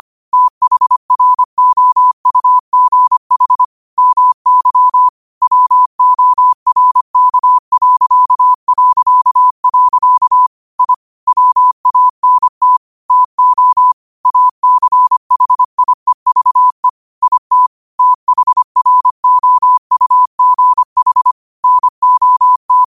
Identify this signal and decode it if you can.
Morse code, heard as a keyed tone, about 25 wpm, 1000 Hz. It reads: THROUGH MY WORK... I WANT TO ACHIEVE IT THROUGH NOT
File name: qotdfiles / 25wpm